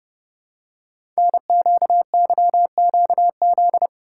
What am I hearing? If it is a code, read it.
Morse code, heard as a keyed tone, about 30 wpm, 700 Hz. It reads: NQYQZ
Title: Code Groups - 5 Letters - 30wpm